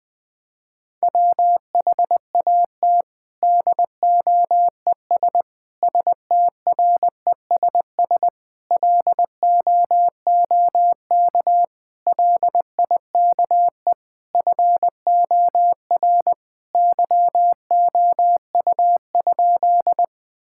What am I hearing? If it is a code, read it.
Morse code, heard as a keyed tone, about 21 wpm, 700 Hz. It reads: WHAT DOES STRESS LOOK LIKE FOR YOU?
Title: Sentences from Top 1000 Words - Review Part 6 - 20wpm